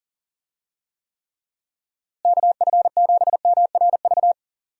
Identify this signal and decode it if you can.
Morse code, heard as a keyed tone, about 40 wpm, 700 Hz. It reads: KF7MRV
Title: USA Call Signs - 6 Characters - Repeated 2x - 40wpm